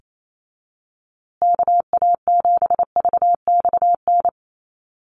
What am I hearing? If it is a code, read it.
Morse code, heard as a keyed tone, about 28 wpm, 700 Hz. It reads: KA7VXN
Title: USA Call Signs Repeated 3x 28wpm